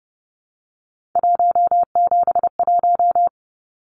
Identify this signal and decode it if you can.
Morse code, heard as a keyed tone, about 30 wpm, 700 Hz. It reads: 171
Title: Numbers - 3-Digit 30wpm